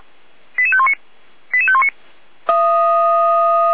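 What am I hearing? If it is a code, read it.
Five-tone selective call (ZVEI), heard as quick runs of five tones, tones of about 70 ms. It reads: 88519 88519
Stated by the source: ZVEI five-tone selective call 88517 received by SDR, from the feuerwehr-orlamuende.org fire brigade audio archive.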